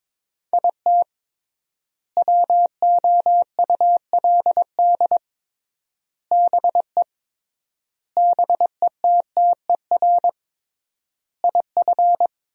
Morse code, heard as a keyed tone, about 22 wpm, 700 Hz. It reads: IT WOULD BE BETTER IF
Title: 3x Word-Spacing - Sentences from Top 400 Words - 22wpm